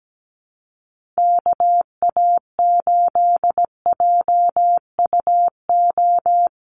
Morse code, heard as a keyed tone, about 17 wpm, 700 Hz. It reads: KA8JUO
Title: USA Call Signs Repeated 2x 17wpm